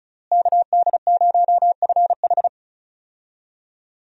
Morse code, heard as a keyed tone, about 36 wpm, 700 Hz. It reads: KD0FH
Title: USA Call Signs - 5 Characters - Repeated 2x - 35wpm